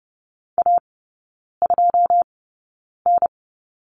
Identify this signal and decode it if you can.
Morse code, heard as a keyed tone, about 30 wpm, spaced as for 9 wpm, 700 Hz. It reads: A2N
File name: 081 Lesson 14e - Number 2 - Three Characters 30wpm